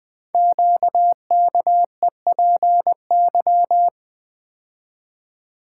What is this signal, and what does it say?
Morse code, heard as a keyed tone, about 20 wpm, 700 Hz. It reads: QKEPY
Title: Code Groups - 5 Letters - 20wpm